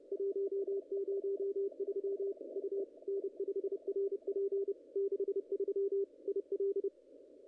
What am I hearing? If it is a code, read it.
Morse code, heard as a keyed tone, about 30 wpm, 410 Hz. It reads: 103 U N5RP 63 IL
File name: contests / ss-cw-k5tr-op-n5rz -0000